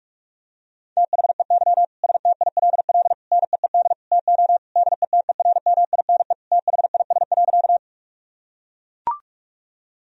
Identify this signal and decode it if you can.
Morse code, heard as a keyed tone, about 42 wpm, 700 Hz. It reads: THEY STILL NEED TO DETERMINE THIS.
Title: Sentences from Top 1000 Words - Review Part 4 - 45wpm